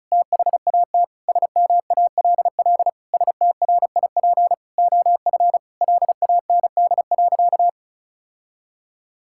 Morse code, heard as a keyed tone, about 34 wpm, 700 Hz. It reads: THAT SMALL STRIP OF LAND.
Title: Sentences from Top 1600 Words 35wpm